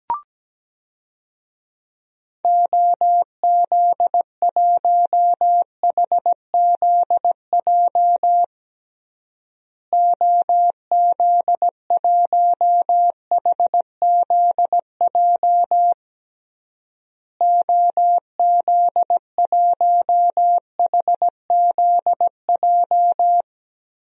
Morse code, heard as a keyed tone, about 17 wpm, 700 Hz. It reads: OZ1HZJ OZ1HZJ OZ1HZJ
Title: DXCC-100 Call Signs - 6 Characters - Repeated 3x - 17wpm